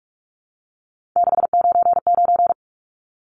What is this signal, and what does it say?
Morse code, heard as a keyed tone, about 44 wpm, 700 Hz. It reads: 699
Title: Numbers - 3-Digit 45wpm